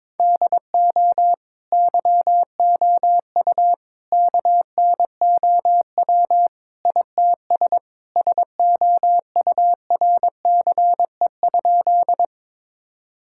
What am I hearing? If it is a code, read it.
Morse code, heard as a keyed tone, about 22 wpm, 700 Hz. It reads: DO YOU KNOW ITS SOURCE?
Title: Sentences from Top 1000 Words - Review Part 5 - 22wpm